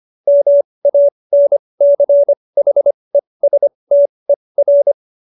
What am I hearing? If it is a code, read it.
Morse code, heard as a keyed tone, about 25 wpm, spaced as for 20 wpm, 570 Hz. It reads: MANCHESTER
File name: QSO313 20